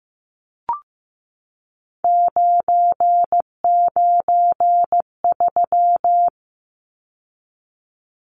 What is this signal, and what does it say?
Morse code, heard as a keyed tone, about 15 wpm, 700 Hz. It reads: 993
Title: Numbers - 3-Digit 15wpm